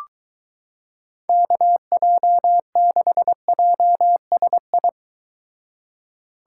Morse code, heard as a keyed tone, about 23 wpm, 700 Hz. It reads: KJ6JSI